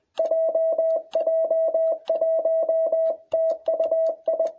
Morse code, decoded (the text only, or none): JJ1TUI